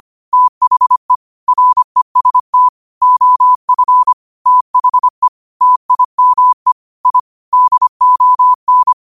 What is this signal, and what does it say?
Morse code, heard as a keyed tone, about 25 wpm, 1000 Hz. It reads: THE REST OF THE TIME I DON
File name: qotdfiles / 25wpm